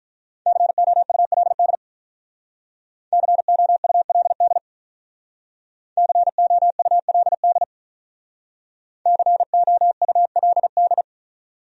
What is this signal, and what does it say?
Morse code, heard as a keyed tone, about 44 wpm, 700 Hz. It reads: COULD COULD COULD COULD
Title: Speed-Racing Top 200 Words 35wpm